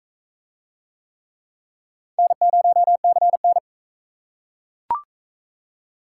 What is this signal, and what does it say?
Morse code, heard as a keyed tone, about 39 wpm, 700 Hz. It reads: N0CN